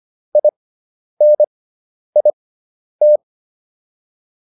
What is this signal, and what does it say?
Morse code, heard as a keyed tone, about 25 wpm, spaced as for 10 wpm, 600 Hz. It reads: INIT